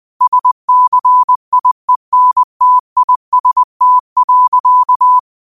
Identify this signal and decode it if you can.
Morse code, heard as a keyed tone, about 20 wpm, 1000 Hz. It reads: SCIENTIST.
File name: qotdfiles / 20wpm